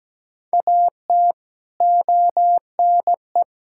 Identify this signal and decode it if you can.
Morse code, heard as a keyed tone, about 17 wpm, 700 Hz. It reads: AT ONE
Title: Sentences from Top 1600 Words 17wpm